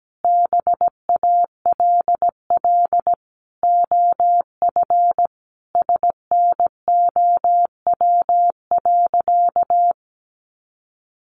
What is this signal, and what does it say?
Morse code, heard as a keyed tone, about 17 wpm, 700 Hz. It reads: BALL OF SNOW.
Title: Easy Phrases 17wpm